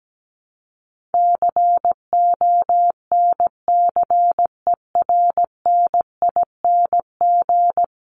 Morse code, heard as a keyed tone, about 17 wpm, 700 Hz. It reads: CONCERNING
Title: Top 1751-2000 Words 17wpm